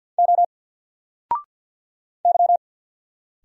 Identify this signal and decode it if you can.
Morse code, heard as a keyed tone, about 49 wpm, 700 Hz. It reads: O Y
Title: Single Letter-Number Alt 50wpm